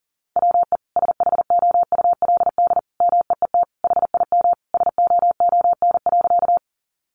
Morse code, heard as a keyed tone, about 40 wpm, 700 Hz. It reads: WE SHOULD MEET HIM SOON.
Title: Sentences from Top 1100 Words 40wpm